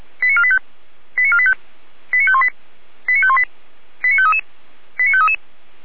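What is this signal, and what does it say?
Five-tone selective call (ZVEI), heard as quick runs of five tones, tones of about 70 ms. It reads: 89485 89485 89518 89518 89533 89533
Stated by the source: ZVEI five-tone selective call received on an SDR.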